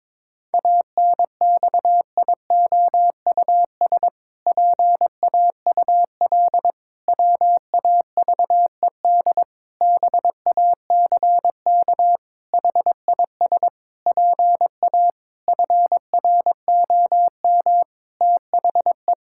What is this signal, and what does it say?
Morse code, heard as a keyed tone, about 22 wpm, 700 Hz. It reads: ANXIOUS PAUL WAVED BACK HIS PA FROM THE
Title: Pangram Challenge v2 - 22wpm